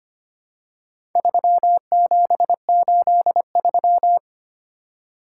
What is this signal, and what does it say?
Morse code, heard as a keyed tone, about 26 wpm, 700 Hz. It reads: 3783
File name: practice / 25wpm / Four Letter-Number Combinations - 25wpm